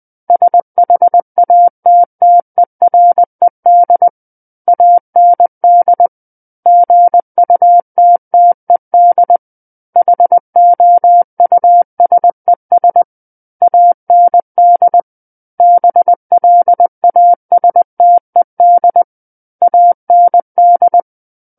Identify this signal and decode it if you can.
Morse code, heard as a keyed tone, about 20 wpm, 700 Hz. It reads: SHATTERED AND GUTTED HOUSES AND BLASTED AND
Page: SKCC Learning Center - Morse Code Practice Files